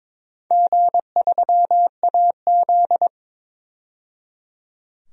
Morse code, heard as a keyed tone, about 22 wpm, 700 Hz. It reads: G3AZ
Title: DXCC-100 Call Signs - 4 Characters - 22wpm